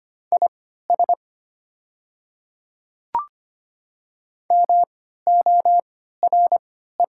Morse code, heard as a keyed tone, about 25 wpm, spaced as for 14 wpm, 700 Hz. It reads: IS MORE
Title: 3x Character Spacing - Top 100 Words - 25wpm